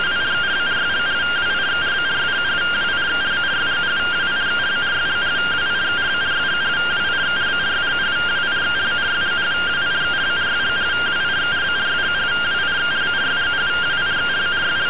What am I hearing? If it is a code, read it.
Two-tone teleprinter signal, about 170 Hz shift.